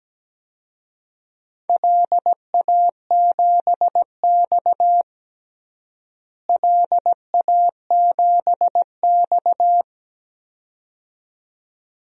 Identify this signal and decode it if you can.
Morse code, heard as a keyed tone, about 17 wpm, 700 Hz. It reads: LA7X LA7X
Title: DXCC-100 Call Signs - 4 Characters - Repeated 2x - 17wpm